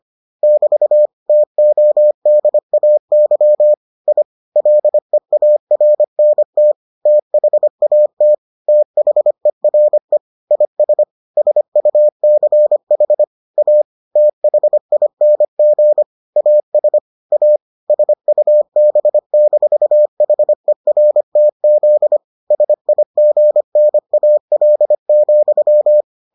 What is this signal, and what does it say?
Morse code, heard as a keyed tone, about 25 wpm, 600 Hz. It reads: <BT> TODAY I LEARNT THAT THERE IS SUCH A THING AS A SUB-HERTZ SIGNAL,